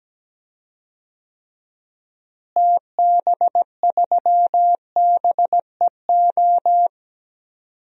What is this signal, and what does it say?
Morse code, heard as a keyed tone, about 17 wpm, 700 Hz. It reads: TB3BEO
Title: DXCC-100 Call Signs - 6 Characters - Repeated 2x - 17wpm